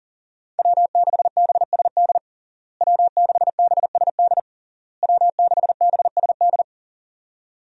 Morse code, heard as a keyed tone, about 40 wpm, 700 Hz. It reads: W6BSD W6BSD W6BSD
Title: USA Call Signs - 5 Characters - Repeated 3x - 40wpm